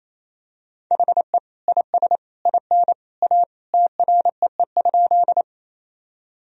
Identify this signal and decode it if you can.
Morse code, heard as a keyed tone, about 27 wpm, 700 Hz. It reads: HE IS IN A TREE?
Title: Sentences from Top 1000 Words - Review Part 1 - 28wpm